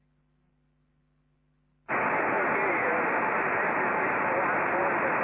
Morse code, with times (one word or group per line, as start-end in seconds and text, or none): none